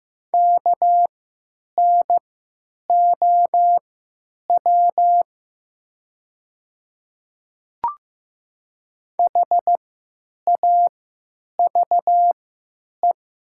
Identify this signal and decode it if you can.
Morse code, heard as a keyed tone, about 15 wpm, spaced as for 9 wpm, 690 Hz. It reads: KNOW HAVE
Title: 3x Character Spacing - Top 100 Words - 15wpm